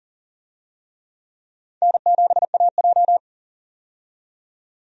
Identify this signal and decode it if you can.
Morse code, heard as a keyed tone, about 40 wpm, 700 Hz. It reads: N7AJ
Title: USA Call Signs - 4 Characters - 40wpm